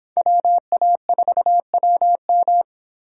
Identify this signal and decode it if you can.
Morse code, heard as a keyed tone, about 29 wpm, 700 Hz. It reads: WA4WM